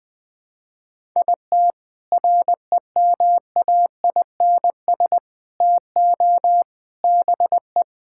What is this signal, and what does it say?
Morse code, heard as a keyed tone, about 20 wpm, 700 Hz. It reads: IT REMAINS TO BE
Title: Sentences with 7-Word Phrases - Spoken Beginning - 20wpm